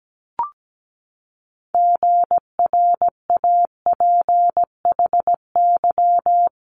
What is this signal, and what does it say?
Morse code, heard as a keyed tone, about 17 wpm, 700 Hz. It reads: GRAPHY
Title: Multi-letter Phonemes 17wpm